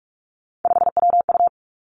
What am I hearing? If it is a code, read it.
Morse code, heard as a keyed tone, about 46 wpm, 700 Hz. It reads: 5WU
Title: Three Challenging Letter-Number Combinations 45wpm